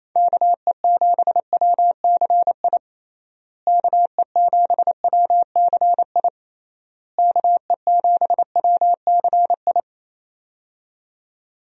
Morse code, heard as a keyed tone, about 28 wpm, 700 Hz. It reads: KE7WCI KE7WCI KE7WCI
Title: USA Call Signs - 6 Characters - Repeated 3x - 28wpm